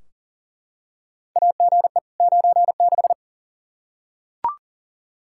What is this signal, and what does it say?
Morse code, heard as a keyed tone, about 38 wpm, 700 Hz. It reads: AGE 96
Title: Practice QSO - Exchange 3 - More Info - 40wpm